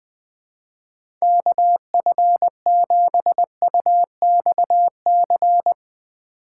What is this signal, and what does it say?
Morse code, heard as a keyed tone, about 20 wpm, 700 Hz. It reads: KF7UXC
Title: K1USN SST QSO Practice - 20wpm